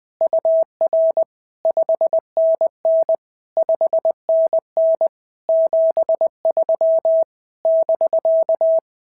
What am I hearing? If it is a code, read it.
Morse code, heard as a keyed tone, about 20 wpm, 650 Hz. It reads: UR 5NN 5NN 73 <BK>
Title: SOTA QSO Practice 20wpm